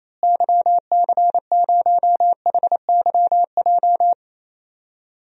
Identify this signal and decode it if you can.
Morse code, heard as a keyed tone, about 28 wpm, 700 Hz. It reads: YC0HYJ
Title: DXCC-100 Call Signs - 6 Characters - Repeated 2x - 28wpm